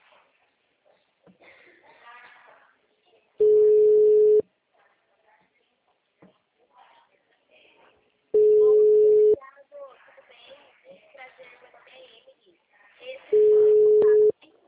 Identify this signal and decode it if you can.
Ringback tone (Europe, 425 Hz), with speech.